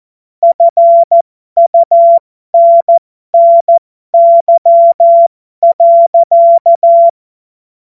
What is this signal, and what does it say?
Morse code, heard as a keyed tone, about 14 wpm, spaced as for 12 wpm, 680 Hz. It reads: FUNNY.
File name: Morsekurs / Witze